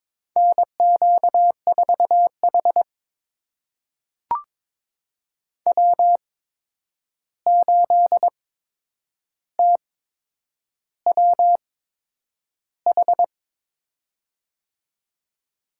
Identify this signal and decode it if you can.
Morse code, heard as a keyed tone, about 22 wpm, spaced as for 9 wpm, 700 Hz. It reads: NQ4H W8TWH
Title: USA Call Signs - 8x Character Spacing - 22wpm